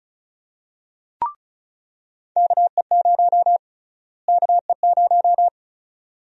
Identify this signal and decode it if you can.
Morse code, heard as a keyed tone, about 36 wpm, 700 Hz. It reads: KE0 KE0
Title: USA Call Sign Prefixes - Repeated 2x - 35wpm